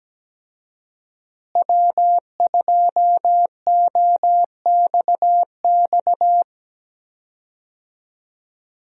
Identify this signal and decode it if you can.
Morse code, heard as a keyed tone, about 17 wpm, 700 Hz. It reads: W2OXX
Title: USA Call Signs - 5 Characters - Repeated 2x - 17wpm